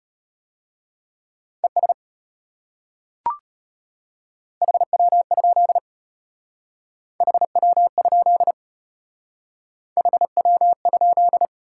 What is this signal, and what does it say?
Morse code, heard as a keyed tone, about 32 wpm, 700 Hz. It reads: ES HW? HW? HW?